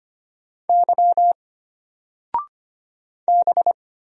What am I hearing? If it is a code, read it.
Morse code, heard as a keyed tone, about 25 wpm, 700 Hz. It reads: Y B
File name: Single-Letter-Number-25wpm